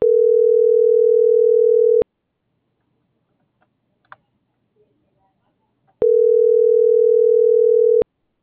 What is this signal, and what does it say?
Ringback tone (North America), 440 + 480 Hz.
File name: CENTRAL TERRESTRE / out karis - .1242